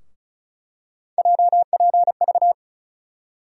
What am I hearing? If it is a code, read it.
Morse code, heard as a keyed tone, about 36 wpm, 700 Hz. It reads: JPV
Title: USA Call Sign Postfixes - 3 Letters 35wpm